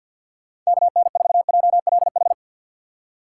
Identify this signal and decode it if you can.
Morse code, heard as a keyed tone, about 61 wpm, 700 Hz. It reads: KN4JLH
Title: Speed Racing - USA 2x3 Call Signs 50wpm